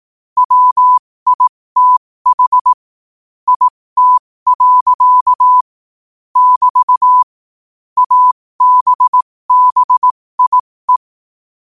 Morse code, heard as a keyed tone, about 18 wpm, spaced as for 15 wpm, 1000 Hz. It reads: WITH IT. <BT> ABBIE
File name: qotdfiles / 15wpm